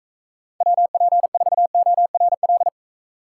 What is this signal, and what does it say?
Morse code, heard as a keyed tone, about 45 wpm, 700 Hz. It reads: WP4ORL